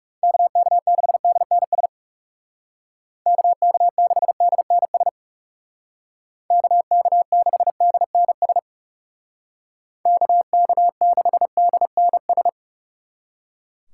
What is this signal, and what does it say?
Morse code, heard as a keyed tone, about 36 wpm, 700 Hz. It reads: KK6DNS KK6DNS KK6DNS KK6DNS
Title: Speed Racing - USA 2x3 Call Signs 30wpm